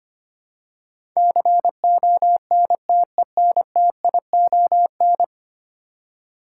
Morse code, heard as a keyed tone, about 25 wpm, 700 Hz. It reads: CONTENTION
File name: 10 Letter Words - First 3 Letters Given - 25wpm